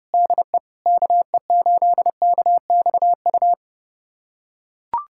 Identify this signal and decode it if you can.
Morse code, heard as a keyed tone, about 30 wpm, 700 Hz. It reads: DE KE8KXU
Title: USA Callsign DE Callsign 30wpm